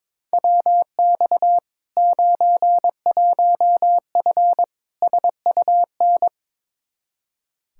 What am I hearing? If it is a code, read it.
Morse code, heard as a keyed tone, about 22 wpm, 700 Hz. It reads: WX 91F SUN
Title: Practice QSO - Exchange 2 - Info - 22wpm